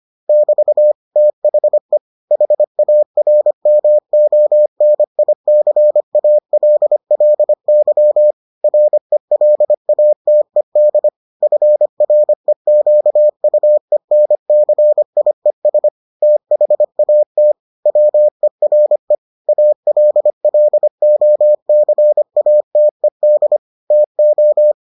Morse code, heard as a keyed tone, about 25 wpm, 600 Hz. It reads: <BT> THE HARMONICALLY RELATED FREQUENCIES THAT WERE ALLOCATED TO